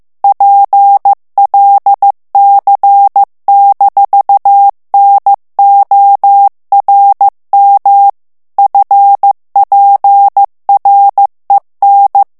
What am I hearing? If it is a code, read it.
Morse code, heard as a keyed tone, about 15 wpm, 800 Hz. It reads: PLC-NORM FPREN
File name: ALLEMAND texte clair 1 vitesse 14 mots minute